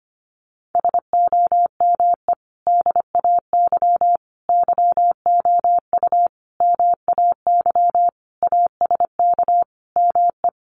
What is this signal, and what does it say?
Morse code, heard as a keyed tone, about 25 wpm, 700 Hz. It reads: SOME DAY YOU MAY ASK ME
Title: Sentences from Top 200 Words 25wpm